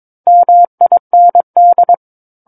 Morse code, heard as a keyed tone, about 22 wpm, 700 Hz. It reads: MIND
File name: 06-Chapter-06 22 WPM